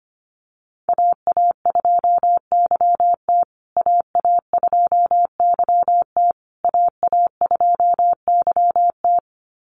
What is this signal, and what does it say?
Morse code, heard as a keyed tone, about 25 wpm, 700 Hz. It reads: AA2YT AA2YT AA2YT
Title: USA Call Signs Repeated 3x 25wpm